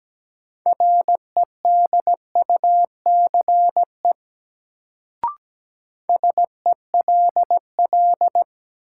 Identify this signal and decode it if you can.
Morse code, heard as a keyed tone, about 17 wpm, 700 Hz. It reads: REDUCE SELL
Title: Sentences from Top 1100 Words - New Word Drill Alt - 17wpm